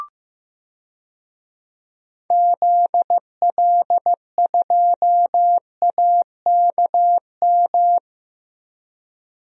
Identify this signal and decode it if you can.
Morse code, heard as a keyed tone, about 15 wpm, 700 Hz. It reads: ZL2AKM
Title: DXCC-100 Call Signs - 6 Characters - 15wpm